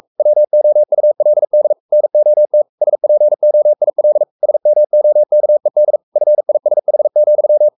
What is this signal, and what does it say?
Morse code, heard as a keyed tone, about 44 wpm, 600 Hz. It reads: WOULD NOT SPOIL SMOKED FISH,
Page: Romanen Never af Ken Follet på engelsk.